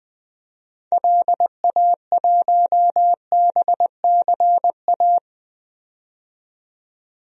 Morse code, heard as a keyed tone, about 21 wpm, 700 Hz. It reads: LA1BCA
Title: DXCC-100 Call Signs - 6 Characters - 20wpm